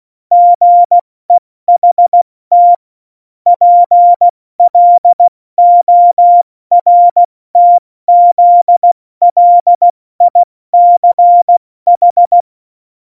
Morse code, heard as a keyed tone, about 16 wpm, 700 Hz. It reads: GEHT PLORTZLICH
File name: Morsekurs / Witze / Witz 29. Januar 2019, 70 F80